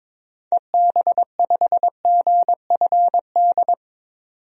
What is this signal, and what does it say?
Morse code, heard as a keyed tone, about 22 wpm, 700 Hz. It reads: EB5GFD
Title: DXCC-100 Call Signs - 6 Characters - 22wpm